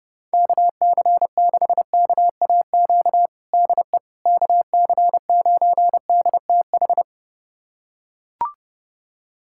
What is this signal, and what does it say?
Morse code, heard as a keyed tone, about 32 wpm, 700 Hz. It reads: KC6KAQ DE KC9DTH
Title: USA Callsign DE Callsign - 6 Characters - 30wpm